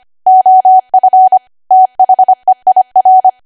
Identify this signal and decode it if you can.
Morse code, heard as a keyed tone, about 25 wpm, 750 Hz. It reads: OF THEIR